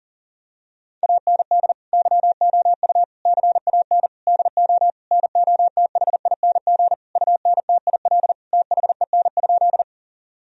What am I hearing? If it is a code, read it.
Morse code, heard as a keyed tone, about 40 wpm, 700 Hz. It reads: AND YOU CAN DO NOTHING UNTIL THEN?
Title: Sentences from Top 1000 Words - Review Part 1 - 40wpm